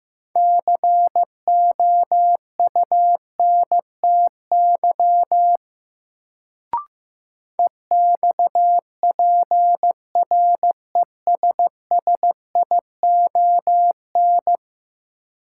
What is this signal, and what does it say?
Morse code, heard as a keyed tone, about 16 wpm, 700 Hz. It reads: COUNTY EXPRESSION
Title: Sentences from Top 1200 Words - New Word Drill Alt - 15wpm